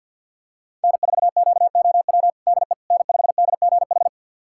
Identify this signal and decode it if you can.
Morse code, heard as a keyed tone, about 52 wpm, 700 Hz. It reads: N4QOW DE N5DGH
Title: USA Callsign DE Callsign - 5 Characters - 50wpm